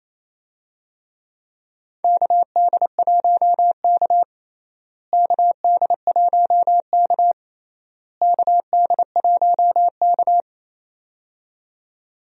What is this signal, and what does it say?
Morse code, heard as a keyed tone, about 28 wpm, 700 Hz. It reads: KD1K KD1K KD1K
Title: USA Call Signs - 4 Characters - Repeated 3x - 28wpm